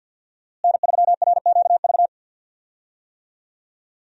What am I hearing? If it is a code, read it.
Morse code, heard as a keyed tone, about 50 wpm, 700 Hz. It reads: N3RQV
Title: USA Call Signs - 5 Characters - Repeated 3x - 50wpm